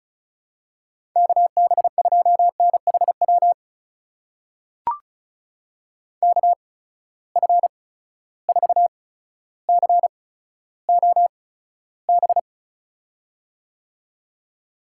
Morse code, heard as a keyed tone, about 36 wpm, spaced as for 15 wpm, 700 Hz. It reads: KB2NHW KF4COB
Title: USA Call Signs - 6 Characters - 8x Character Spacing - 35wpm